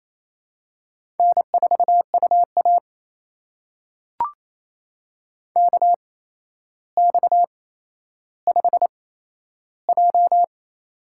Morse code, heard as a keyed tone, about 28 wpm, spaced as for 12 wpm, 700 Hz. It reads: N4UA KX5J
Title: USA Call Signs - 4 Characters - 8x Character Spacing - 28wpm